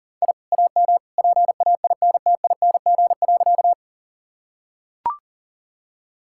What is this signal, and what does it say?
Morse code, heard as a keyed tone, about 40 wpm, 700 Hz. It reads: I AM PAINTING.
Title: Sentences from Top 1000 Words - Review Part 6 - 40wpm